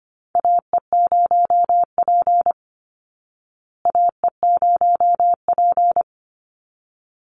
Morse code, heard as a keyed tone, about 25 wpm, 700 Hz. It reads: AE0P AE0P
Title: Call Signs Repeated 3x - 25wpm